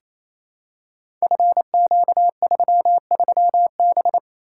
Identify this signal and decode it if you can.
Morse code, heard as a keyed tone, about 27 wpm, 700 Hz. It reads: FQ33B
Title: Code Groups - 5 Letter:Numbers - 28wpm